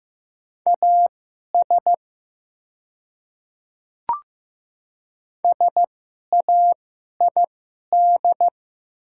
Morse code, heard as a keyed tone, about 15 wpm, spaced as for 11 wpm, 700 Hz. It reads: AS SAID
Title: 2x Character Spacing - Top 100 Words - 15wpm